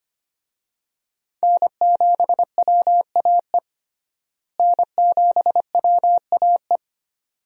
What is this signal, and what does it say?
Morse code, heard as a keyed tone, about 25 wpm, 700 Hz. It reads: N7WAE N7WAE
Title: USA Call Signs - 5 Characters - Repeated 2x - 25wpm